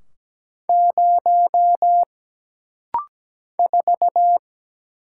Morse code, heard as a keyed tone, about 17 wpm, 700 Hz. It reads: 0 4
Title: Single Letter-Number Alt - Rapid-Fire 17wpm